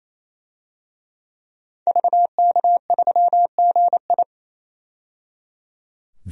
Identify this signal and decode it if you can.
Morse code, heard as a keyed tone, about 28 wpm, 700 Hz. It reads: VK3GI
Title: DXCC-100 Call Signs - 5 Characters - 28wpm